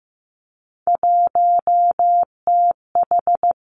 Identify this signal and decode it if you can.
Morse code, heard as a keyed tone, about 15 wpm, 700 Hz. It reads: 1TH
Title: Three Challenging Letter-Number Combinations 15wpm